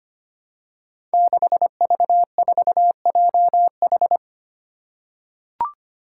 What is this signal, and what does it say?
Morse code, heard as a keyed tone, about 25 wpm, 700 Hz. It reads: 6V4JH
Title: Five Challenging Letter-Number Combinations 25wpm